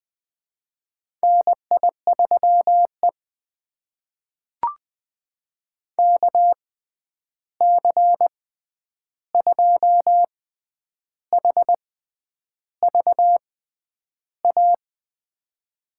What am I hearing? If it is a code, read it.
Morse code, heard as a keyed tone, about 20 wpm, spaced as for 9 wpm, 700 Hz. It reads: NI3E KC2HVA
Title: USA Call Signs - 6x Character Spacing - 20wpm